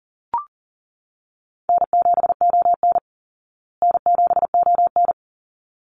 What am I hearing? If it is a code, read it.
Morse code, heard as a keyed tone, about 40 wpm, 700 Hz. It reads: N7ON N7ON
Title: Call Signs Repeated 3x - 40wpm